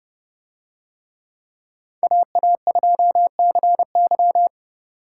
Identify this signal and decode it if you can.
Morse code, heard as a keyed tone, about 29 wpm, 700 Hz. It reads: AA2CY